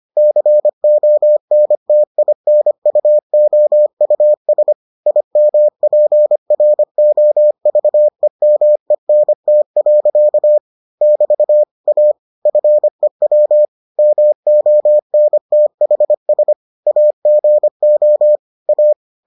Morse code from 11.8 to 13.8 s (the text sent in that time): A FEW